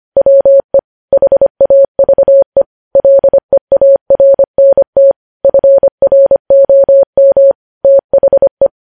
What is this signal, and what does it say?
Morse code, heard as a keyed tone, about 25 wpm, 550 Hz. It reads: WE HAVE LEARNT FROM THE
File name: podcast / News headlines-1x-25x25